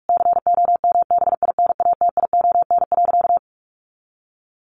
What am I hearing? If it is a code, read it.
Morse code, heard as a keyed tone, about 46 wpm, 700 Hz. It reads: COMBINATION.
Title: Sentences from Top 1300 Words 45wpm